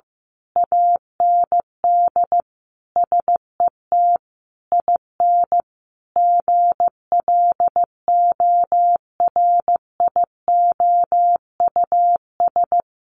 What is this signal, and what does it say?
Morse code, heard as a keyed tone, about 15 wpm, 700 Hz. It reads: AND SET IN GLORIOUS
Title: A Sound Of Thunder 15wpm